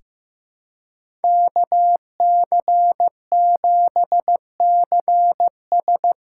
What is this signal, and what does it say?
Morse code, heard as a keyed tone, about 15 wpm, 700 Hz. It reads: KC7CS
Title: USA Call Signs - 6 Characters - 6x Character Spacing - 15wpm